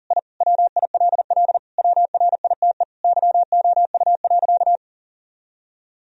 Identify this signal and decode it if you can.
Morse code, heard as a keyed tone, about 40 wpm, 700 Hz. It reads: I WILL WRITE YOU.
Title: Sentences from Top 1000 Words - Review Part 1 - 40wpm